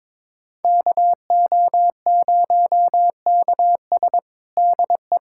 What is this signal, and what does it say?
Morse code, heard as a keyed tone, about 22 wpm, 700 Hz. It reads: KO0KS DE
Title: USA Callsign DE Callsign 22wpm